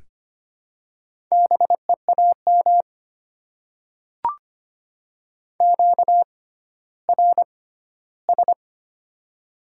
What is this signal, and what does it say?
Morse code, heard as a keyed tone, about 25 wpm, spaced as for 14 wpm, 700 Hz. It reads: BEAM QRS
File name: Top 90 QSO Elements - 6x Character Spacing - 25wpm